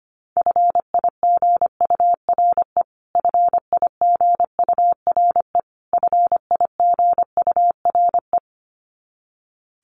Morse code, heard as a keyed tone, about 25 wpm, 700 Hz. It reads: FIGURE FIGURE FIGURE
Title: Top 500 Words Repeated 25wpm